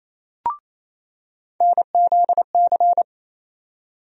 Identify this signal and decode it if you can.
Morse code, heard as a keyed tone, about 28 wpm, 700 Hz. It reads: NZC
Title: USA Call Sign Postfixes - 3 Letters 28wpm